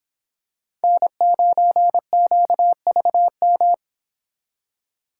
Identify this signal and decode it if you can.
Morse code, heard as a keyed tone, about 26 wpm, 700 Hz. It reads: N9QVM
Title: Speed Racing - USA Call Signs 22wpm